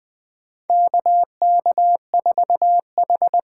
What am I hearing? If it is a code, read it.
Morse code, heard as a keyed tone, about 20 wpm, 700 Hz. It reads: KK4H
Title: USA Call Signs - 6 Characters - 8x Character Spacing - 20wpm